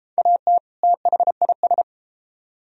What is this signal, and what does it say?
Morse code, heard as a keyed tone, about 33 wpm, 700 Hz. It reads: AT THIS